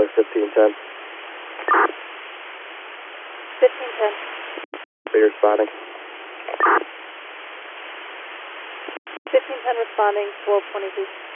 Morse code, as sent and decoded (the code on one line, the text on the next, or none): none